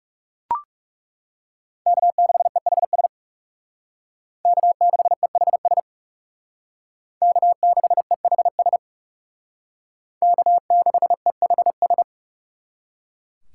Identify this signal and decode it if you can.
Morse code, heard as a keyed tone, about 36 wpm, 700 Hz. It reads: K6EHS K6EHS K6EHS K6EHS
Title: Speed Racing - USA 1x3 Call Signs 30wpm